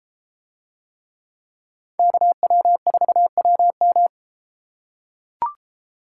Morse code, heard as a keyed tone, about 32 wpm, 700 Hz. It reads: KW4WM